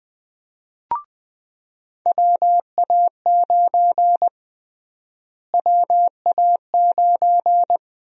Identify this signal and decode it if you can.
Morse code, heard as a keyed tone, about 20 wpm, 700 Hz. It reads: WA9 WA9
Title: USA Call Sign Prefixes - Repeated 2x - 20wpm